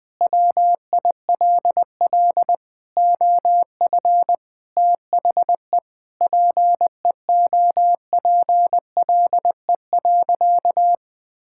Morse code, heard as a keyed tone, about 21 wpm, 700 Hz. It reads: WILL OF THE PEOPLE.
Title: Sentences from Top 1000 Words - Review Part 5 - 20wpm